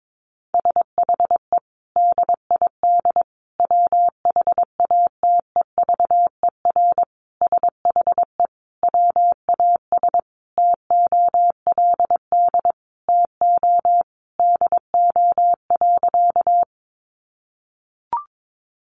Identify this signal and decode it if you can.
Morse code, heard as a keyed tone, about 22 wpm, 700 Hz. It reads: SHE DID WHATEVER SHE WAS TOLD TO DO.
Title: Sentences from Top 1100 Words 22wpm